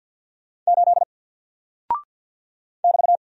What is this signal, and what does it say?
Morse code, heard as a keyed tone, about 48 wpm, 700 Hz. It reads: 8 <BT>
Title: Single Character (Letter:Number:Punctuation:Prosign) Alt 50wpm